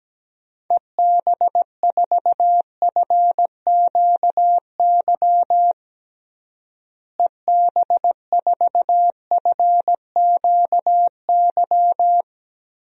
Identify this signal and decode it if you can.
Morse code, heard as a keyed tone, about 17 wpm, 700 Hz. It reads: EB4FQY EB4FQY
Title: DXCC-100 Call Signs - 6 Characters - Repeated 3x - 17wpm